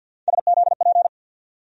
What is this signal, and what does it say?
Morse code, heard as a keyed tone, about 51 wpm, 690 Hz. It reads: SZP
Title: USA Call Sign Postfixes - 3 Letters 50wpm